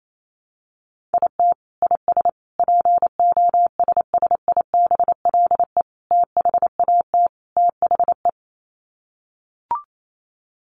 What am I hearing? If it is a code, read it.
Morse code, heard as a keyed tone, about 28 wpm, 700 Hz. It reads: IT IS POSSIBLE THAT THE
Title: Sets of 5 Words Encore2 28wpm